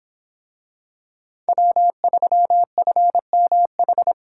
Morse code, heard as a keyed tone, about 24 wpm, 700 Hz. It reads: W3FMH